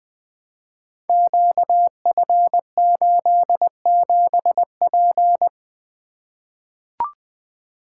Morse code, heard as a keyed tone, about 20 wpm, 700 Hz. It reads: QF87P
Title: Code Groups - 5 Letter:Numbers - 20wpm